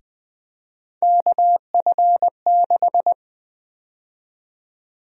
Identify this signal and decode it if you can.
Morse code, heard as a keyed tone, about 20 wpm, 700 Hz. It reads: KF6
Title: USA Call Sign Prefixes - Repeated 2x - 20wpm